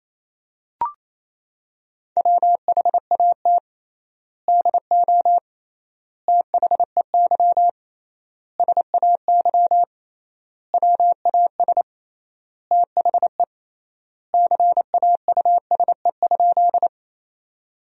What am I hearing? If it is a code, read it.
Morse code, heard as a keyed tone, about 28 wpm, 700 Hz. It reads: WHAT DO THEY SAY WAS THE CAUSE?
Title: 3x Word-Spacing - Sentences from Top 200 Words - 28wpm